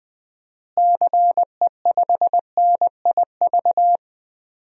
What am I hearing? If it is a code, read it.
Morse code, heard as a keyed tone, about 20 wpm, 700 Hz. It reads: CE5NIV
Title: DXCC-100 Call Signs - 6 Characters - Repeated 3x - 20wpm